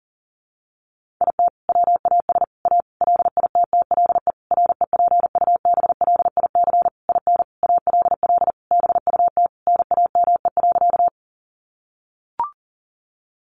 Morse code, heard as a keyed tone, about 40 wpm, 700 Hz. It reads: IT WAS A LITTLE REPUBLIC IN ALL BUT NAME.
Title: Sentences from Top 1500 Words 40wpm